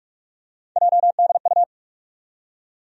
Morse code, heard as a keyed tone, about 46 wpm, 700 Hz. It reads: JDU
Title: USA Call Sign Postfixes - 3 Letters 45wpm